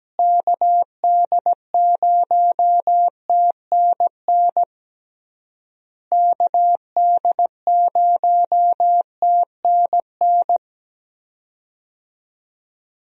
Morse code, heard as a keyed tone, about 17 wpm, 700 Hz. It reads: KD0TNN KD0TNN
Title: USA Call Signs - 6 Characters - Repeated 3x - 17wpm